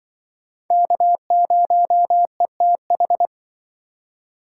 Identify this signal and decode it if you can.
Morse code, heard as a keyed tone, about 24 wpm, 700 Hz. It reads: K0ETH
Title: Speed Racing - USA 1x3 Call Signs 20wpm